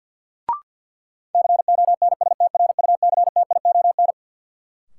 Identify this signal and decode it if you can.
Morse code, heard as a keyed tone, about 47 wpm, 700 Hz. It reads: CONSTRUCTION
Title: Sentences from Top 1600 Words - New Word Drill - 50wpm